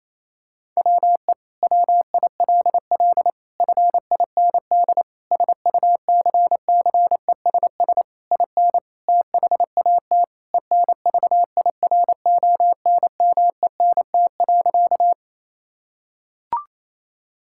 Morse code, heard as a keyed tone, about 28 wpm, 700 Hz. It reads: WE WILL FIND SUCCESS IN THAT ENVIRONMENT.
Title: Sentences from Top 1000 Words - Review Part 5 - 28wpm